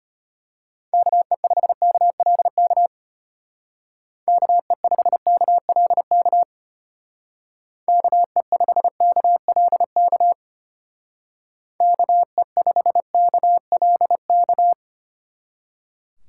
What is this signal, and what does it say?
Morse code, heard as a keyed tone, about 30 wpm, 700 Hz. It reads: KE5KLK KE5KLK KE5KLK KE5KLK
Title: Speed Racing - USA 2x3 Call Signs 25wpm